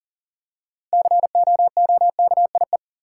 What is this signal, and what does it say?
Morse code, heard as a keyed tone, about 40 wpm, 700 Hz. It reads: COOKIE
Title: Six Letter Words - 40wpm